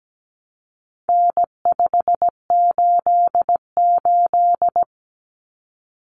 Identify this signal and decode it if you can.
Morse code, heard as a keyed tone, about 17 wpm, 700 Hz. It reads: N588
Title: Common Cut Numbers - 4-Digit 17wpm